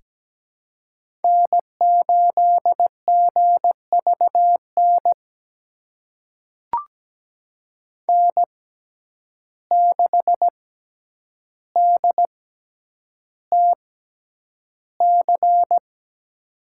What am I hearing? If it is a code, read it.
Morse code, heard as a keyed tone, about 17 wpm, spaced as for 9 wpm, 700 Hz. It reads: N8GVN N6DTC
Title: USA Call Signs - 5 Characters - 6x Character Spacing - 17wpm